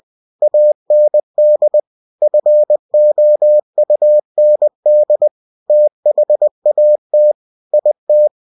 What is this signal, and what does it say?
Morse code, heard as a keyed tone, about 20 wpm, 600 Hz. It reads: AND FOUND THAT IT
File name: cw-books / winnie-the-pooh / winnie-20wpm-page-0024